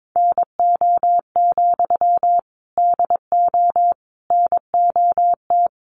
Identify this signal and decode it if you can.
Morse code, heard as a keyed tone, about 22 wpm, 700 Hz. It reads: NO, DO NOT
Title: Sentences from Top 1200 Words 22wpm